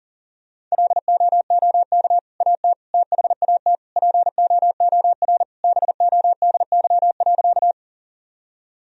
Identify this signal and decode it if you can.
Morse code, heard as a keyed tone, about 40 wpm, 700 Hz. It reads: LOOK AT THAT POOR BODY.
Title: Sentences from Top 1000 Words - Review Part 2 - 40wpm